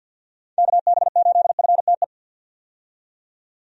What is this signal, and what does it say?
Morse code, heard as a keyed tone, about 52 wpm, 700 Hz. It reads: KB8FTE
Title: USA Call Signs - 6 Characters - 50wpm